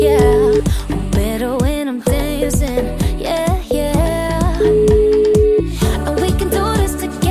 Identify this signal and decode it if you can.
Ringback tone (Europe, 425 Hz), with noise.